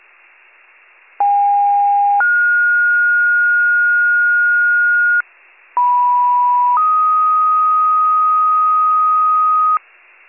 Two-tone page, heard as a one-second tone, then a three-second tone, about 810 then 1430 Hz, about 980 then 1250 Hz. It is heard from 1.2 to 9.8 s.